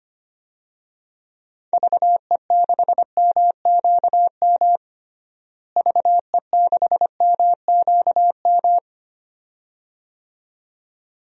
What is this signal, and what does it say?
Morse code, heard as a keyed tone, about 25 wpm, 700 Hz. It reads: VE6MQM VE6MQM
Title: DXCC-100 Call Signs - 6 Characters - Repeated 2x - 25wpm